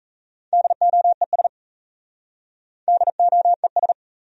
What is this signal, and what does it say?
Morse code, heard as a keyed tone, about 38 wpm, 700 Hz. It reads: DOES DOES